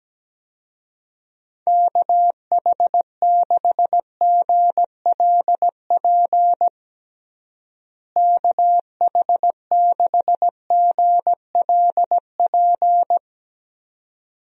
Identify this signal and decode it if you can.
Morse code, heard as a keyed tone, about 17 wpm, 700 Hz. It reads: KH6GLP KH6GLP
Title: DXCC-100 Call Signs - 6 Characters - Repeated 2x - 17wpm